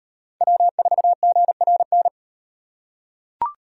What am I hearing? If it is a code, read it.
Morse code, heard as a keyed tone, about 35 wpm, 700 Hz. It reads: W4GRN